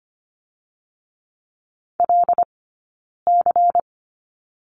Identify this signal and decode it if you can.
Morse code, heard as a keyed tone, about 25 wpm, 700 Hz. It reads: L C
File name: Lesson 17d - Letter u - Two Characters 25wpm